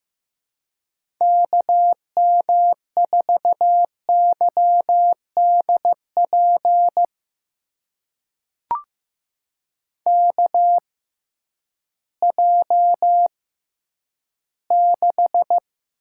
Morse code, heard as a keyed tone, about 15 wpm, spaced as for 10 wpm, 700 Hz. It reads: KM4YDP KJ6
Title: USA Call Signs - 6 Characters - 6x Character Spacing - 15wpm